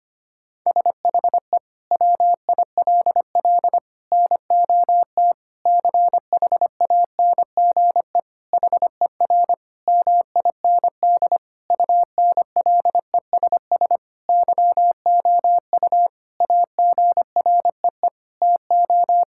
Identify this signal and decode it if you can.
Morse code, heard as a keyed tone, about 25 wpm, 700 Hz. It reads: SHE WILL NOT CHANGE HER MIND UNLESS YOU AGREE TO
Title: Sentences from Top 1000 Words - Review Part 6 - 25wpm